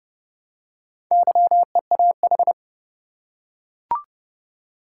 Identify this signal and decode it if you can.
Morse code, heard as a keyed tone, about 30 wpm, 700 Hz. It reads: YEAH
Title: Four Letter Words - 30wpm